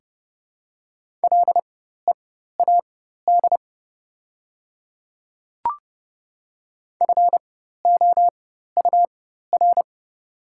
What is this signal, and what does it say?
Morse code, heard as a keyed tone, about 30 wpm, spaced as for 14 wpm, 700 Hz. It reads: LEAD FOUR
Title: 4x Character Spacing - Top 500 Words - 30wpm